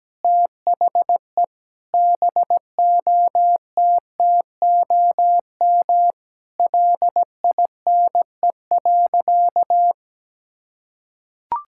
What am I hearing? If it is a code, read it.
Morse code, heard as a keyed tone, about 17 wpm, 700 Hz. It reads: THE BOTTOM LINE.
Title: Sentences from Top 1000 Words - Review Part 6 - 17wpm